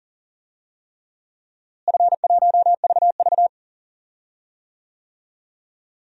Morse code, heard as a keyed tone, about 40 wpm, 700 Hz. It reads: F1VV
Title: DXCC-100 Call Signs - 4 Characters - Repeated 3x - 40wpm